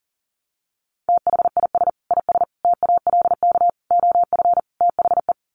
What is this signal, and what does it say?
Morse code, heard as a keyed tone, about 40 wpm, 700 Hz. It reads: THIS IS TALK OF THE
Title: Sets of 5 Words 40wpm